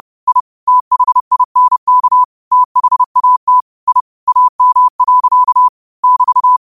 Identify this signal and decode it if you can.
Morse code, heard as a keyed tone, about 30 wpm, 1000 Hz. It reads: I THINK THAT I AM. <BT>
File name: 2026-02-23-30wpm